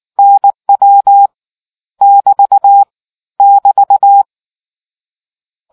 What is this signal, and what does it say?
Morse code, heard as a keyed tone, about 19 wpm, 800 Hz. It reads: NW <BT> <BT>